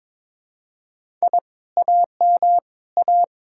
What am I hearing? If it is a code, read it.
Morse code, heard as a keyed tone, about 22 wpm, 700 Hz. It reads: I AM A
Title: Speed-Racing Common QSO Phrases - 22wpm